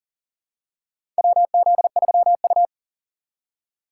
Morse code, heard as a keyed tone, about 40 wpm, 690 Hz. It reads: WZ3U
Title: USA Call Signs - 4 Characters - 2x Character Spacing - 40wpm